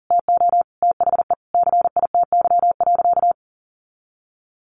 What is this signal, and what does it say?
Morse code, heard as a keyed tone, about 40 wpm, 700 Hz. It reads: TO THE CITY.
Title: Sentences from Top 1400 Words 40wpm